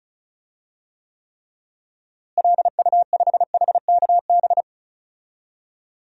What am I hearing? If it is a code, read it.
Morse code, heard as a keyed tone, about 34 wpm, 700 Hz. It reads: LU5HKB
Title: DXCC-100 Call Signs - 6 Characters - Repeated 3x - 35wpm